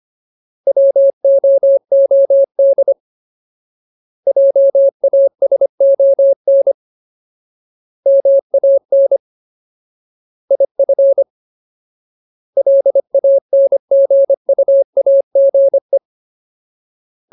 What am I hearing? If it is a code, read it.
Morse code, heard as a keyed tone, about 25 wpm, 550 Hz. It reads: WOOD JASON MAN IF LANGUAGE
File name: Ultimate Word List / 25WPM Ultimate